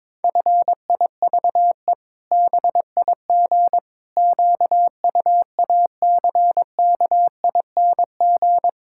Morse code, heard as a keyed tone, about 22 wpm, 700 Hz. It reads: FIVE BIG QUACKING
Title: Pangram Challenge v2 - 22wpm